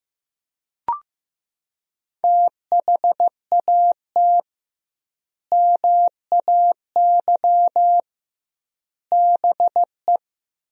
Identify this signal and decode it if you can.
Morse code, heard as a keyed tone, about 15 wpm, 700 Hz. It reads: THAT MAY BE
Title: 2x Word-Spacing - Sentences from Top 400 Words - 15wpm